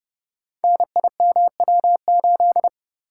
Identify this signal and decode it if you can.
Morse code, heard as a keyed tone, about 30 wpm, 700 Hz. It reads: NIMW8
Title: Code Groups - 5 Letter:Numbers - 30wpm